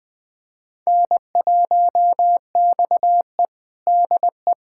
Morse code, heard as a keyed tone, about 20 wpm, 700 Hz. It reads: N1XE DE
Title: USA Callsign DE Callsign - 4 Characters - 20wpm